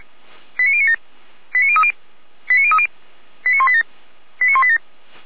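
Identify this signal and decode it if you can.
Five-tone selective call (ZVEI), heard as quick runs of five tones, tones of about 70 ms. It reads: 89097 89030 89030 89187 89187
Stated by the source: ZVEI five-tone selective call received on SDR.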